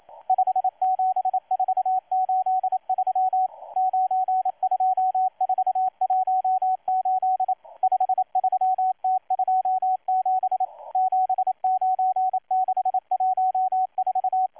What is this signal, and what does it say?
Morse code, heard as a keyed tone, about 27 wpm, 750 Hz, with radio noise. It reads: 57483 92418 53T27 79614
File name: M12-2016-07-07-1310utc-13926khz-error-restart